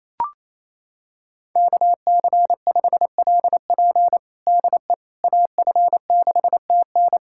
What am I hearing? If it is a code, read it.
Morse code, heard as a keyed tone, about 28 wpm, 700 Hz. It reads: KC5LP DE AF6TN
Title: USA Callsign DE Callsign - 5 Characters - 28wpm